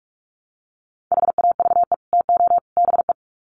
Morse code, heard as a keyed tone, about 44 wpm, 700 Hz. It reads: HAVE TO BE
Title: Sets of 3 Words 45wpm